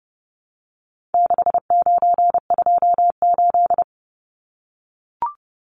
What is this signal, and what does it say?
Morse code, heard as a keyed tone, about 30 wpm, 700 Hz. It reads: 6928
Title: Numbers - 4-Digit 30wpm